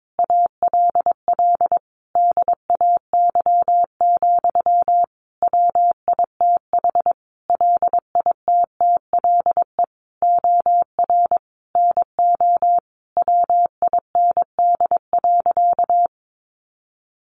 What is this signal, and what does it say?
Morse code, heard as a keyed tone, about 22 wpm, 700 Hz. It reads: ALL DAY, WITH LITTLE OR NO WIND.
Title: Sentences from Top 1400 Words 22wpm